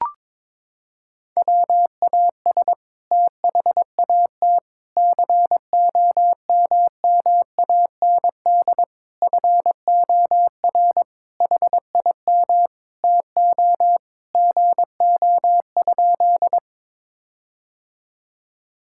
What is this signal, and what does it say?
Morse code, heard as a keyed tone, about 22 wpm, 700 Hz. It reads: WAS THAT COMMAND FOR HIM TO GO?
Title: Sentences from Top 1600 Words 22wpm